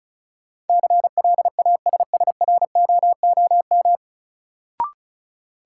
Morse code, heard as a keyed tone, about 34 wpm, 700 Hz. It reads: CLASSROOM
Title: Sentences from Top 1500 Words - New Word Drill Alt - 35wpm